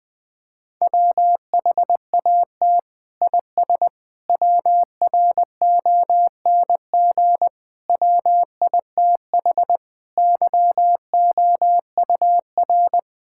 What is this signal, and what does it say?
Morse code, heard as a keyed tone, about 20 wpm, 700 Hz. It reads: WHAT IS WRONG WITH YOUR
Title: Sentences from Top 1000 Words - Review Part 5 - 20wpm